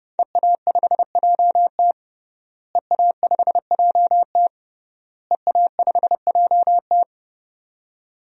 Morse code, heard as a keyed tone, about 30 wpm, 700 Hz. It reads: EA5JT EA5JT EA5JT
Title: DXCC-100 Call Signs - 5 Characters - Repeated 3x - 30wpm